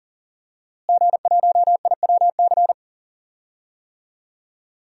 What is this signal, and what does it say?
Morse code, heard as a keyed tone, about 38 wpm, 700 Hz. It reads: G1IWC
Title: DXCC-100 Call Signs - 5 Characters - Repeated 3x - 40wpm